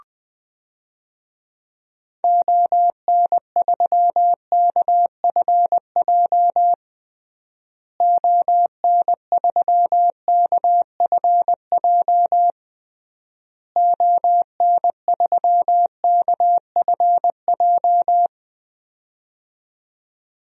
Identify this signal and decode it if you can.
Morse code, heard as a keyed tone, about 20 wpm, 700 Hz. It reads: ON3KFJ ON3KFJ ON3KFJ
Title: DXCC-100 Call Signs - 6 Characters - Repeated 3x - 20wpm